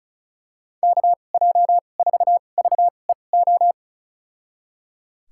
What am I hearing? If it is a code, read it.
Morse code, heard as a keyed tone, about 34 wpm, spaced as for 25 wpm, 700 Hz. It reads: KJ4VEO
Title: USA Call Signs - 6 Characters - 2x Character Spacing - 35wpm